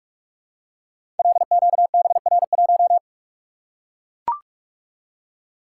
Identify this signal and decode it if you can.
Morse code, heard as a keyed tone, about 44 wpm, 700 Hz. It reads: LQBR1
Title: Code Groups - 5 Letter:Numbers - 45wpm